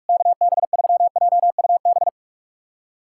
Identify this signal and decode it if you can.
Morse code, heard as a keyed tone, about 43 wpm, 700 Hz. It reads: KB3JUB
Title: USA Callsign DE Callsign - 6 Characters - 45wpm